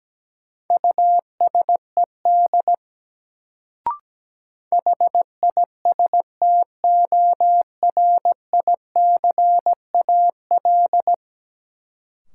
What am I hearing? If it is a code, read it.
Morse code, heard as a keyed tone, about 17 wpm, 700 Hz. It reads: USED HISTORICAL
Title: Sentences from Top 600 Words - New Word Drill - 17wpm